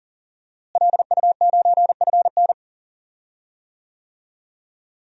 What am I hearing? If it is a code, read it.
Morse code, heard as a keyed tone, about 38 wpm, 700 Hz. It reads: LU9FN
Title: DXCC-100 Call Signs - 5 Characters - Repeated 3x - 40wpm